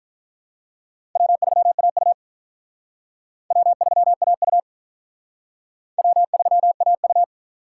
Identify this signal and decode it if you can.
Morse code, heard as a keyed tone, about 43 wpm, 700 Hz. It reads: W3AU W3AU W3AU
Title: Speed Racing - USA 1x2 Call Signs 35wpm